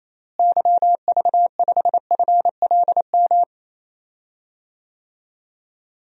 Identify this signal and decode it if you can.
Morse code, heard as a keyed tone, about 28 wpm, 700 Hz. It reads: YV5FLM
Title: DXCC-100 Call Signs - 6 Characters - Repeated 3x - 28wpm